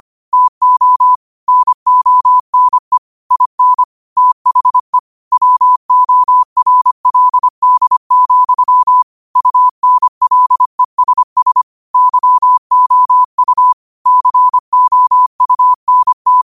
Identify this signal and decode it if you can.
Morse code, heard as a keyed tone, about 25 wpm, 1000 Hz. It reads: TO NONE IN THE WORLD, UNLESS YOU COUNT
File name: qotdfiles / 25wpm